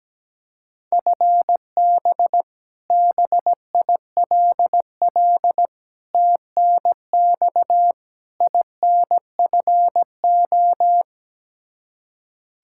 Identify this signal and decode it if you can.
Morse code, heard as a keyed tone, about 17 wpm, 700 Hz. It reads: FB BILL TNX INFO
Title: Practice QSO - Exchange 3 - More Info - 17wpm